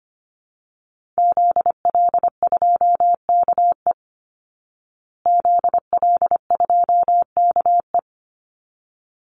Morse code, heard as a keyed tone, about 25 wpm, 700 Hz. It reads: ZL2KE ZL2KE
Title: Call Signs Repeated 3x - 25wpm